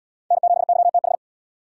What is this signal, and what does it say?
Morse code, heard as a keyed tone, about 75 wpm, 700 Hz. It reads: N7YTH